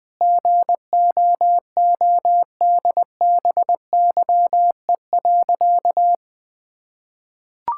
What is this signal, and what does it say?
Morse code, heard as a keyed tone, about 20 wpm, 700 Hz. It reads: GOODBYE.
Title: Sentences with 7-Word Phrases - 20wpm